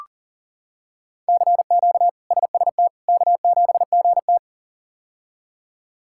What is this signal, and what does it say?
Morse code, heard as a keyed tone, about 38 wpm, 700 Hz. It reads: CQ SST K7GT
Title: K1USN SST QSO Practice - 40wpm